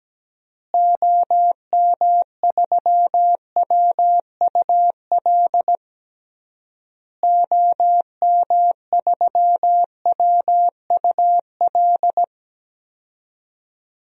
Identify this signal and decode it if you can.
Morse code, heard as a keyed tone, about 17 wpm, 700 Hz. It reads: OM3WUL OM3WUL
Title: DXCC-100 Call Signs - 6 Characters - Repeated 3x - 17wpm